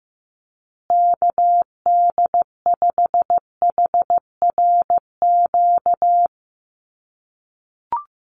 Morse code, heard as a keyed tone, about 15 wpm, 700 Hz. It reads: KD5HRQ
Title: USA Call Signs Repeated 3x 15wpm